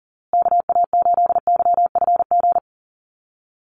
Morse code, heard as a keyed tone, about 40 wpm, 700 Hz. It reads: KA8YFG
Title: USA Call Signs Repeated 2x 40wpm